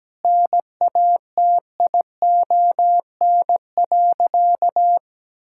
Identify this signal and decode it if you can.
Morse code, heard as a keyed tone, about 17 wpm, 700 Hz. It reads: NATION.
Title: Sentences from Top 1000 Words - Review Part 6 - 17wpm